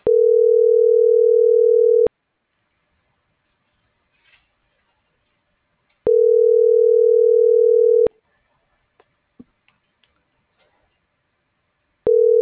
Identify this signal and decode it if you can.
Ringback tone (North America), 440 + 480 Hz.